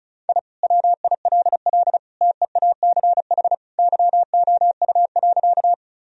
Morse code, heard as a keyed tone, about 36 wpm, 700 Hz. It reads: I WILL TEACH YOU.
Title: Sentences from Top 1000 Words - Review Part 2 - 35wpm